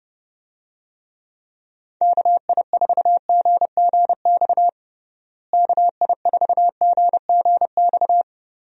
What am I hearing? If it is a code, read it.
Morse code, heard as a keyed tone, about 32 wpm, 700 Hz. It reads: KI4GGX KI4GGX
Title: USA Call Signs - 6 Characters - Repeated 3x - 30wpm